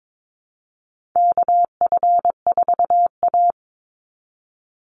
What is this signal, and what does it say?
Morse code, heard as a keyed tone, about 22 wpm, 700 Hz. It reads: KF4A
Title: Call Signs Repeated 2x - 22wpm